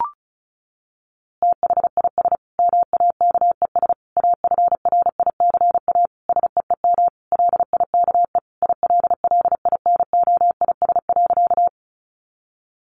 Morse code, heard as a keyed tone, about 34 wpm, 700 Hz. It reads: THIS MAKES AFRICA SEEM LIKE ILLINOIS.
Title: A Sound Of Thunder 35wpm